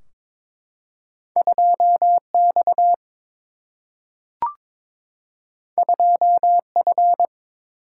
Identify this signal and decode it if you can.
Morse code, heard as a keyed tone, about 22 wpm, 700 Hz. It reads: 2X 2F
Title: Single-Letter Single-Number 22wpm